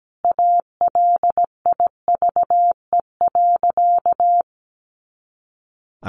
Morse code, heard as a keyed tone, about 17 wpm, 700 Hz. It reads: ALIVE.
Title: A Sound Of Thunder 17wpm